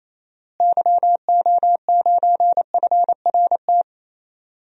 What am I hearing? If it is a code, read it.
Morse code, heard as a keyed tone, about 28 wpm, 700 Hz. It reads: YO9FRT
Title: DXCC-100 Call Signs - 6 Characters - Repeated 3x - 28wpm